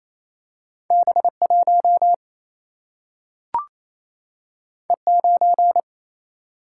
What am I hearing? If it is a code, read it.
Morse code, heard as a keyed tone, about 28 wpm, 700 Hz. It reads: B1 E9
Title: Single-Letter Single-Number 28wpm